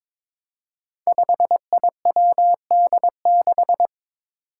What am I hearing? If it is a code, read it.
Morse code, heard as a keyed tone, about 23 wpm, 700 Hz. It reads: 5IWD6
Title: Five Challenging Letter-Number Combinations 22wpm